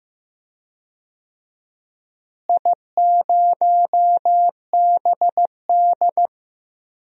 Morse code, heard as a keyed tone, about 15 wpm, 700 Hz. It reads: I0BD
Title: DXCC-100 Call Signs - 4 Characters - Repeated 2x - 15wpm